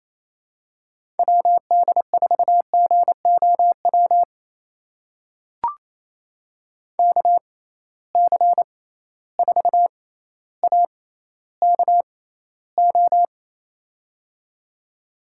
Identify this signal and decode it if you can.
Morse code, heard as a keyed tone, about 28 wpm, spaced as for 14 wpm, 700 Hz. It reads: WD4GOW KC4AKO
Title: USA Call Signs - 6 Characters - 6x Character Spacing - 28wpm